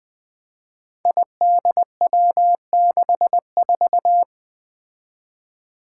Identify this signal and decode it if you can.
Morse code, heard as a keyed tone, about 21 wpm, 700 Hz. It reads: IDW64
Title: Five Challenging Letter-Number Combinations 20wpm